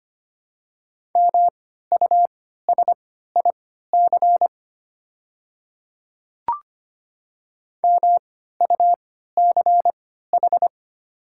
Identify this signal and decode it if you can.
Morse code, heard as a keyed tone, about 25 wpm, spaced as for 14 wpm, 700 Hz. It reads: MUSIC MUCH
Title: 3x Character Spacing - Top 300 Words - 25wpm